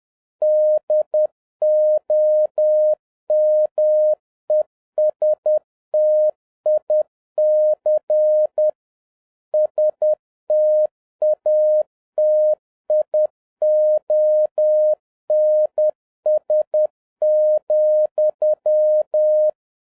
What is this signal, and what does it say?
Morse code, heard as a keyed tone, about 10 wpm, 610 Hz. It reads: DOMESTIC STATIONS,